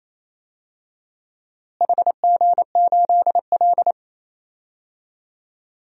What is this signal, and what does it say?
Morse code, heard as a keyed tone, about 28 wpm, 700 Hz. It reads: HG8L
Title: DXCC-100 Call Signs - 4 Characters - Repeated 3x - 28wpm